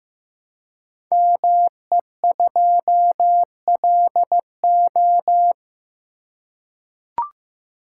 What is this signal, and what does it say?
Morse code, heard as a keyed tone, about 15 wpm, 700 Hz. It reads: ME2LO
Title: Code Groups - 5 Letter:Numbers - 15wpm